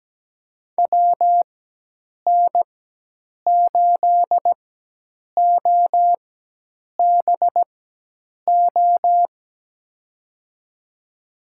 Morse code, heard as a keyed tone, about 17 wpm, spaced as for 8 wpm, 700 Hz. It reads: WN8OBO
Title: USA Call Signs - 6 Characters - 4x Character Spacing - 17wpm